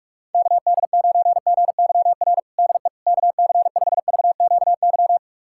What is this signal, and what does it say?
Morse code, heard as a keyed tone, about 46 wpm, 700 Hz. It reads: KD9GYR DE KC5VQY
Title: USA Callsign DE Callsign - 6 Characters - 45wpm